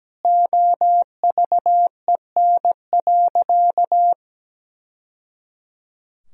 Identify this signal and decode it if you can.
Morse code, heard as a keyed tone, about 17 wpm, 700 Hz. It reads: OVEN.
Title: Sentences from Top 1000 Words - Review Part 4 - 17wpm